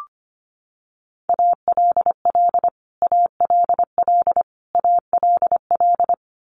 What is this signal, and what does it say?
Morse code, heard as a keyed tone, about 25 wpm, 700 Hz. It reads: ALL ALL ALL
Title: Top 200 Words Repeated 25wpm